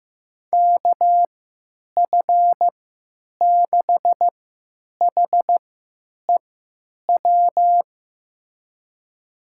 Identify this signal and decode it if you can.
Morse code, heard as a keyed tone, about 15 wpm, spaced as for 9 wpm, 700 Hz. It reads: KF6HEW
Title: USA Call Signs - 3x Character Spacing - 15wpm